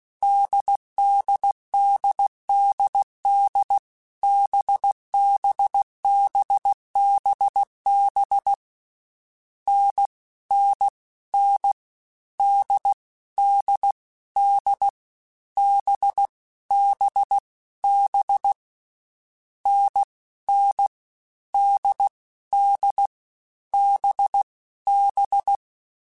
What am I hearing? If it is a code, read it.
Morse code, heard as a keyed tone, about 16 wpm, 800 Hz. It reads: DDDDD BBBBB N N N D D D B B B N N D D B B
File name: Difficult Combos / Difficult Combos-mp3s / NDB NDB6 @15